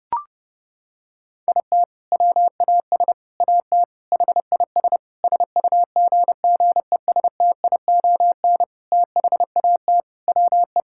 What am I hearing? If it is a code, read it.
Morse code, heard as a keyed tone, about 30 wpm, 700 Hz. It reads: IT WAS AT HIS SUGGESTION THAT WE
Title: Sentences from Top 1600 Words 30wpm